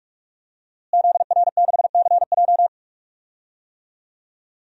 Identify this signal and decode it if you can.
Morse code, heard as a keyed tone, about 45 wpm, 700 Hz. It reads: ZR6CJ
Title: DXCC-100 Call Signs - 5 Characters - Repeated 2x - 45wpm